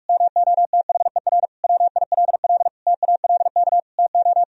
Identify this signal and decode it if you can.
Morse code, heard as a keyed tone, about 42 wpm, 700 Hz. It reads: MOTHER WILL TALK TO
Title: Sentences from Top 1000 Words - Review Part 1 - 45wpm